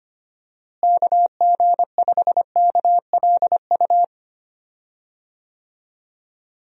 Morse code, heard as a keyed tone, about 25 wpm, 700 Hz. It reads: KG5KLU
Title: USA Call Signs - 6 Characters - Repeated 2x - 25wpm